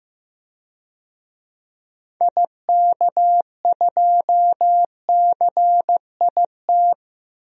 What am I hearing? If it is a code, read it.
Morse code, heard as a keyed tone, about 15 wpm, 700 Hz. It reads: IK2CIT
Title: DXCC-100 Call Signs - 6 Characters - 15wpm